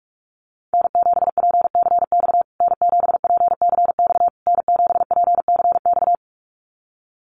Morse code, heard as a keyed tone, about 44 wpm, 700 Hz. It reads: N7PCX N7PCX N7PCX
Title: USA Call Signs Repeated 3x 45wpm